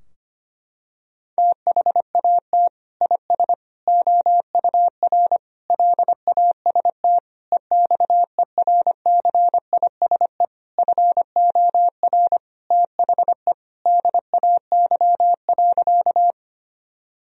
Morse code, heard as a keyed tone, about 25 wpm, 700 Hz. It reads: THAT IS OUR LAST EXERCISE FOR THE DAY.
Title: Sentences from Top 1000 Words - Review Part 4 - 25wpm